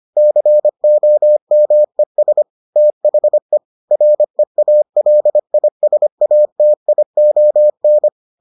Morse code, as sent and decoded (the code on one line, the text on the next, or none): -.-. --- -- . ... / - .... / .-. .- .-.. .. ... .- - .. --- -.
COMES THE REALISATION